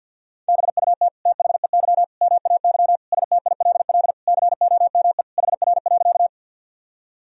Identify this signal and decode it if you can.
Morse code, heard as a keyed tone, about 50 wpm, 700 Hz. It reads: BUT THEY MAY STILL COME HR.
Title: Sentences from Top 300 Words and Abbreviations 50wpm